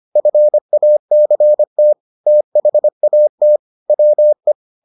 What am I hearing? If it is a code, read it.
Morse code, heard as a keyed tone, about 25 wpm, 600 Hz. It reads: FACT THAT WE